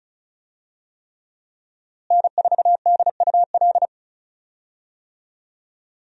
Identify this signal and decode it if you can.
Morse code, heard as a keyed tone, about 34 wpm, 700 Hz. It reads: N4DUL
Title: USA Call Signs - 5 Characters - Repeated 3x - 35wpm